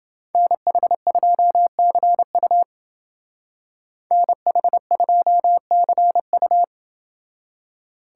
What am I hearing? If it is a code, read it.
Morse code, heard as a keyed tone, about 29 wpm, 700 Hz. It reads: NH2CU NH2CU